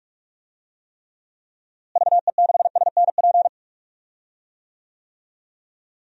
Morse code, heard as a keyed tone, about 46 wpm, 700 Hz. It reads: VE6SNP
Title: DXCC-100 Call Signs - 6 Characters - Repeated 2x - 45wpm